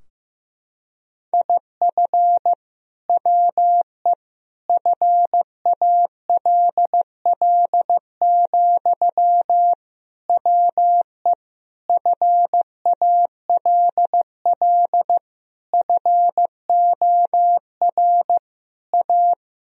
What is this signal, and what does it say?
Morse code, heard as a keyed tone, about 16 wpm, 700 Hz. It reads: IF WE FALL, WE FALL FOR A
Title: Sentences from Top 1000 Words - Review Part 2 - 15wpm